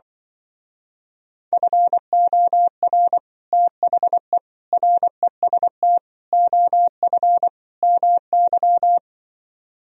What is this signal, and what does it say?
Morse code, heard as a keyed tone, about 24 wpm, 700 Hz. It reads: FOR THE REST OF MY
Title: Speed-Racing - Sets of 5 Words 20wpm